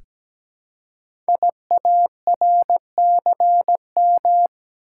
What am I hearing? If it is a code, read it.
Morse code, heard as a keyed tone, about 17 wpm, 700 Hz. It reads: IARCM
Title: Code Groups - 5 Letters - 17wpm